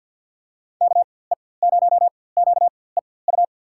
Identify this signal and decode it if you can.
Morse code, heard as a keyed tone, about 47 wpm, spaced as for 23 wpm, 700 Hz. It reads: KE0QEU
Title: USA Call Signs - 6 Characters - 4x Character Spacing - 50wpm